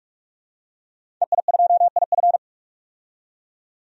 Morse code, heard as a keyed tone, about 46 wpm, 700 Hz. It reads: EI2IF
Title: DXCC-100 Call Signs - 5 Characters - 45wpm